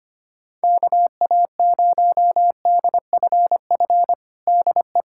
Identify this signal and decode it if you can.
Morse code, heard as a keyed tone, about 25 wpm, 700 Hz. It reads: KA0DFF DE
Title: USA Callsign DE Callsign 25wpm